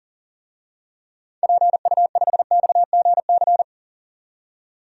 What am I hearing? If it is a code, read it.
Morse code, heard as a keyed tone, about 40 wpm, 700 Hz. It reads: PU5XGC
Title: DXCC-100 Call Signs - 6 Characters - 40wpm